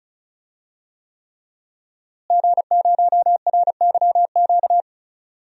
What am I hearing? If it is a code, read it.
Morse code, heard as a keyed tone, about 36 wpm, 700 Hz. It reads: G0RYQ
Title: DXCC-100 Call Signs - 5 Characters - Repeated 3x - 35wpm